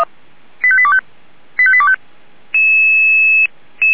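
Five-tone selective call (ZVEI), heard as quick runs of five tones, tones of about 70 ms. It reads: 85825 85825
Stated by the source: ZVEI five-tone selective call received on SDR.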